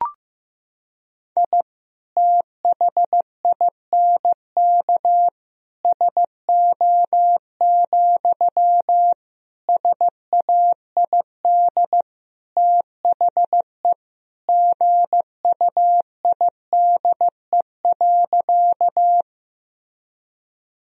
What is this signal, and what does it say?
Morse code, heard as a keyed tone, about 15 wpm, 700 Hz. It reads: I THINK SO, SAID THE GUIDE.
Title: Sentences from Top 1000 Words - Review Part 3 - 15wpm